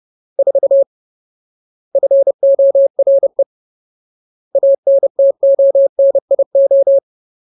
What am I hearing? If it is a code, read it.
Morse code, heard as a keyed tone, about 30 wpm, 550 Hz. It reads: 4 FORE ANTONIO
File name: Ultimate Word List / 30WPM Ultimate